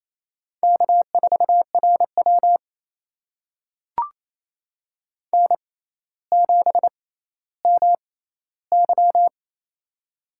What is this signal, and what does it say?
Morse code, heard as a keyed tone, about 28 wpm, spaced as for 14 wpm, 700 Hz. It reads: K4RW N7MY
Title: USA Call Signs - 6x Character Spacing - 28wpm